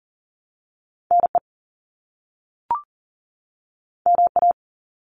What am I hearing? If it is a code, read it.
Morse code, heard as a keyed tone, about 39 wpm, 700 Hz. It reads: NE MA
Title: US States 40wpm